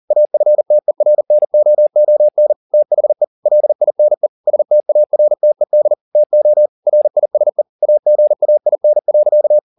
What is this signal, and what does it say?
Morse code, heard as a keyed tone, about 40 wpm, 600 Hz. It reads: AFTERNOON THE LINE STARTED TO RISE AGAIN.